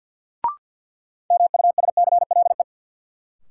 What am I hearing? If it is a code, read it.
Morse code, heard as a keyed tone, about 47 wpm, 700 Hz. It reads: MUSCLE
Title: Sentences from Top 1300 Words - New Word Drill - 50wpm